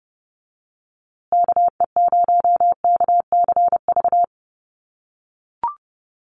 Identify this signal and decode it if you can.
Morse code, heard as a keyed tone, about 30 wpm, 700 Hz. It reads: KE0KCV
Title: USA Call Signs Repeated 3x 30wpm